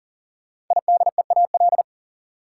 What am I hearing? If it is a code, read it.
Morse code, heard as a keyed tone, about 40 wpm, 700 Hz. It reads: IDEAL
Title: Sentences from Top 1400 Words - New Word Drill - 40wpm